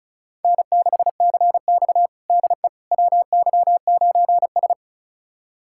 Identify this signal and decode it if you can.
Morse code, heard as a keyed tone, about 36 wpm, 700 Hz. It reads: N6CX DE WY9S
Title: USA Callsign DE Callsign - 4 Characters - 35wpm